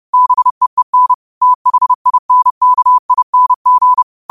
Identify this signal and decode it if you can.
Morse code, heard as a keyed tone, about 30 wpm, 1000 Hz. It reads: BEEN THINKING
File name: qotdfiles / 30wpm